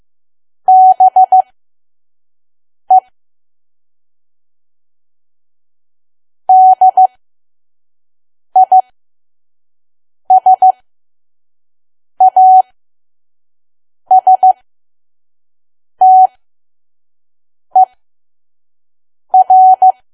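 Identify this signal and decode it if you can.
Morse code, heard as a keyed tone, about 15 wpm, spaced as for 5 wpm, 750 Hz. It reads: BE DISASTER
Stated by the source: Morse code practice at 5 WPM from KQ2P.COM.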